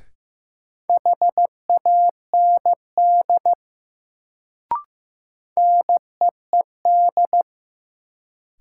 Morse code, heard as a keyed tone, about 15 wpm, 700 Hz. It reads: HAND NEED
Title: Sentences from Top 200 Words - New Word Drill - 15wpm